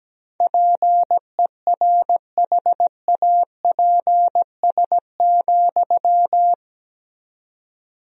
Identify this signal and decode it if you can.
Morse code, heard as a keyed tone, about 17 wpm, 700 Hz. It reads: PERHAPS,
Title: 4x Word-Spacing - Sentences from Top 500 Words - 17wpm